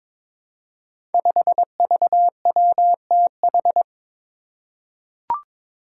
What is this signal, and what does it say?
Morse code, heard as a keyed tone, about 22 wpm, 700 Hz. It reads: 5VWTH
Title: Five Challenging Letter-Number Combinations 22wpm